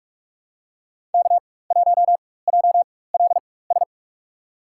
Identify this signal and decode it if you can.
Morse code, heard as a keyed tone, about 46 wpm, spaced as for 21 wpm, 700 Hz. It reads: K1JLS
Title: USA Call Signs - 5 Characters - 4x Character Spacing - 45wpm